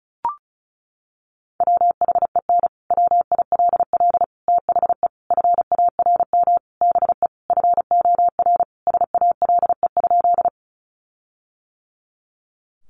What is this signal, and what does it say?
Morse code, heard as a keyed tone, about 34 wpm, 700 Hz. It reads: WHEN WILL THE FARM BE FOR SALE?
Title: Sentences from Top 1400 Words 35wpm